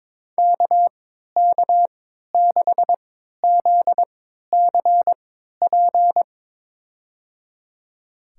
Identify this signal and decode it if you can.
Morse code, heard as a keyed tone, about 22 wpm, spaced as for 13 wpm, 700 Hz. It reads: KK6ZCP
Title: USA Call Signs - 3x Character Spacing - 22wpm